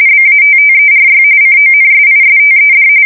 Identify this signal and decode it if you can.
RTTY, about 45.45 baud. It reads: BATTERY CONSISTED
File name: fsk170 / battleshipnj